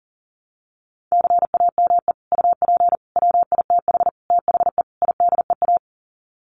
Morse code, heard as a keyed tone, about 40 wpm, 700 Hz. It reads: CAME UP WITH THE IDEA
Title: Sets of 5 Words Encore2 40wpm